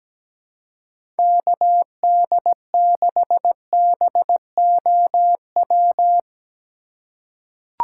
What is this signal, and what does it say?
Morse code, heard as a keyed tone, about 17 wpm, 700 Hz. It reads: KD6BOW
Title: USA Call Signs - 2x Character Spacing - 17wpm